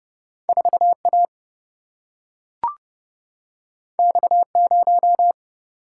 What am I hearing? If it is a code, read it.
Morse code, heard as a keyed tone, about 30 wpm, 700 Hz. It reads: 4A X0
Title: Single-Letter Single-Number 30wpm